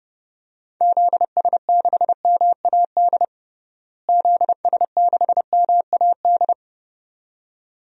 Morse code, heard as a keyed tone, about 30 wpm, 700 Hz. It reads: ZS6MAD ZS6MAD
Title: DXCC-100 Call Signs - 6 Characters - Repeated 2x - 30wpm